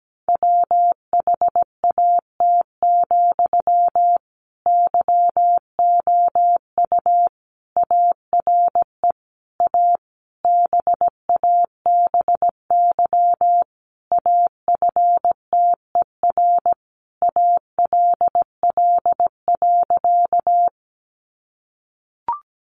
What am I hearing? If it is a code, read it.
Morse code, heard as a keyed tone, about 17 wpm, 700 Hz. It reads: WHAT, YOU ARE A BABY AFTER ALL.
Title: Sentences from Top 1300 Words 17wpm